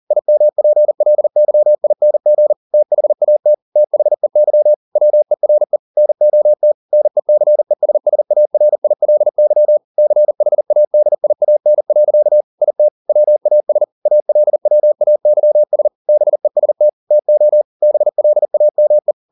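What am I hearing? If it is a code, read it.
Morse code, heard as a keyed tone, about 40 wpm, 600 Hz. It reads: IMPLYING THAT THEY WERE NOT NECESSARILY CHADIAN. IT WAS ALWAYS BEST TO BLAME
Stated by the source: Morse code at 40 wpm, practice text, CW med Gnister.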